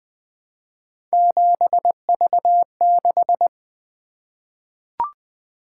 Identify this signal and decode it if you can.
Morse code, heard as a keyed tone, about 20 wpm, 700 Hz. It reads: 7V6
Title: Three Letter-Number Combinations - 20wpm